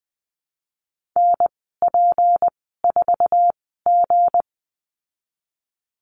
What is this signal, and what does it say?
Morse code, heard as a keyed tone, about 19 wpm, spaced as for 14 wpm, 700 Hz. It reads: NP4G
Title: 161 Lesson 26g - Letter g - Call Signs 20wpm